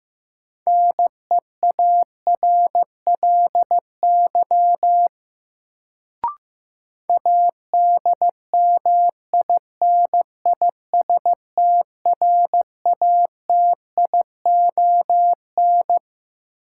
Morse code, heard as a keyed tone, about 15 wpm, 700 Hz. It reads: NEARLY ADMINISTRATION
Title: Sentences from Top 1300 Words - New Word Drill Alt - 15wpm